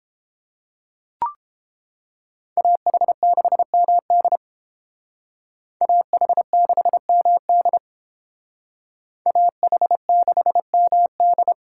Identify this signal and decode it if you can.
Morse code, heard as a keyed tone, about 30 wpm, 700 Hz. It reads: AH6MD AH6MD AH6MD